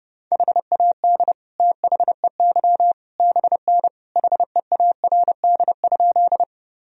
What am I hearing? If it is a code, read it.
Morse code, heard as a keyed tone, about 32 wpm, 700 Hz. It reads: HAD THEY BN HEARD?
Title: Sentences from Top 400 Words and Abbreviations 30wpm